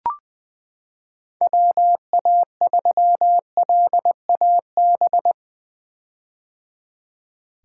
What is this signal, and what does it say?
Morse code, heard as a keyed tone, about 20 wpm, 700 Hz. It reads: WA3LAB
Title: K1USN SST QSO Practice - 20wpm